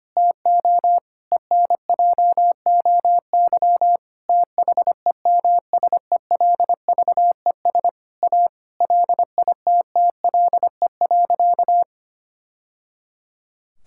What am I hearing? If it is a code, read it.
Morse code, heard as a keyed tone, about 25 wpm, 700 Hz. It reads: TO ENJOY THEMSELVES A LITTLE.
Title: Sentences from Top 1000 Words - Review Part 5 - 25wpm